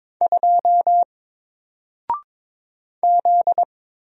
Morse code, heard as a keyed tone, about 22 wpm, 700 Hz. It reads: 2 Z
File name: Single-Letter-Number-22wpm